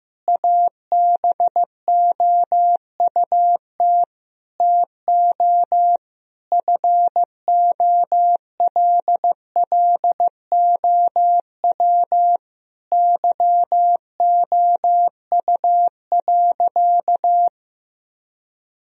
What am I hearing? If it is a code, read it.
Morse code, heard as a keyed tone, about 16 wpm, 700 Hz. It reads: ABOUT TO FOLLOW YOU.
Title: Sentences from Top 1000 Words - Review Part 1 - 15wpm